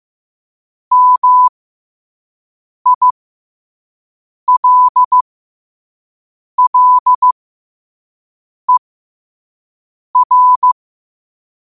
Morse code, heard as a keyed tone, about 15 wpm, spaced as for 5 wpm, 1000 Hz. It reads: MILLER